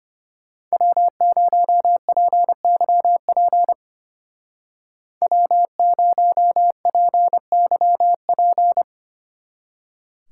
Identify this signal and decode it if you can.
Morse code, heard as a keyed tone, about 32 wpm, 700 Hz. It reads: W0PYP W0PYP